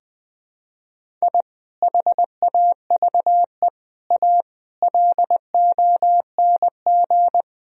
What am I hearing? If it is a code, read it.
Morse code, heard as a keyed tone, about 21 wpm, 700 Hz. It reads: I HAVE A LONG
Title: Sentences from Top 1000 Words - Review Part 1 - 20wpm